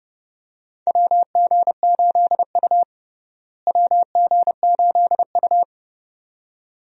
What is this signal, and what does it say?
Morse code, heard as a keyed tone, about 30 wpm, 700 Hz. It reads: WG8U WG8U
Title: USA Call Signs - 4 Characters - Repeated 2x - 30wpm